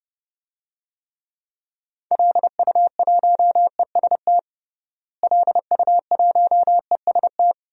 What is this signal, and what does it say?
Morse code, heard as a keyed tone, about 30 wpm, 700 Hz. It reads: LU1EST LU1EST
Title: DXCC-100 Call Signs - 6 Characters - Repeated 3x - 30wpm